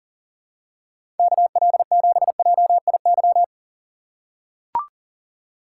Morse code, heard as a keyed tone, about 40 wpm, 700 Hz. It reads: KL7JIY
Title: USA Call Signs - 6 Characters - 4x Character Spacing - 40wpm